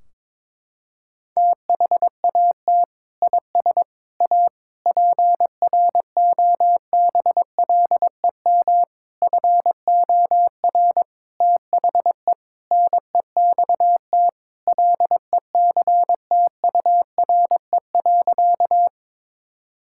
Morse code, heard as a keyed tone, about 21 wpm, 700 Hz. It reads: THAT IS A PROBLEM FOR THE NEXT LECTURE.
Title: Sentences from Top 1600 Words 22wpm